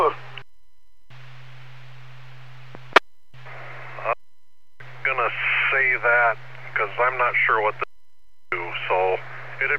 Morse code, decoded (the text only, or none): none